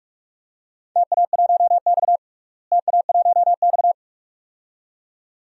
Morse code, heard as a keyed tone, about 46 wpm, 700 Hz. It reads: TA1X TA1X
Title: DXCC-100 Call Signs - 4 Characters - Repeated 2x - 45wpm